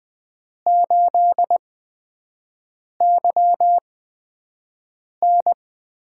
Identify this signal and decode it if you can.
Morse code, heard as a keyed tone, about 20 wpm, spaced as for 5 wpm, 700 Hz. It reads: 8YN
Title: USA Call Signs - 6 Characters - 8x Character Spacing - 20wpm